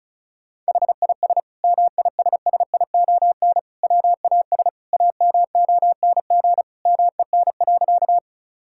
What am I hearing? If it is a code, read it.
Morse code, heard as a keyed tone, about 36 wpm, 700 Hz. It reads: HIS MISSION WAS AMONG MEN.
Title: Sentences from Top 1600 Words 35wpm